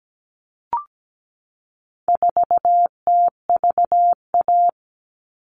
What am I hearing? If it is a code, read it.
Morse code, heard as a keyed tone, about 17 wpm, 700 Hz. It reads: 4TVA
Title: All Cut Numbers - 4-Digit 17wpm